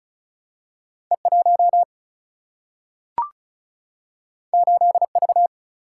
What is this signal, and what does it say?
Morse code, heard as a keyed tone, about 36 wpm, 700 Hz. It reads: E1 8V
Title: Single-Letter Single-Number 35wpm